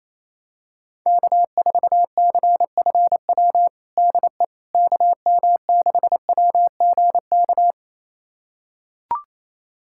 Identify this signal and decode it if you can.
Morse code, heard as a keyed tone, about 28 wpm, 700 Hz. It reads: K4CFW DE KM6WGK
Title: USA Callsign DE Callsign 28wpm